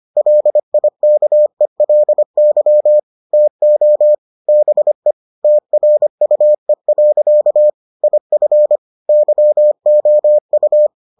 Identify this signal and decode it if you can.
Morse code, heard as a keyed tone, about 25 wpm, 600 Hz. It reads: LIKELY TO BE TRUE. IF YOU